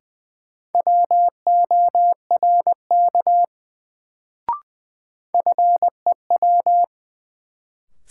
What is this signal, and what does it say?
Morse code, heard as a keyed tone, about 20 wpm, 700 Hz. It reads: WORK FEW
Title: Sets of 2 Words - New Word Drill - 20wpm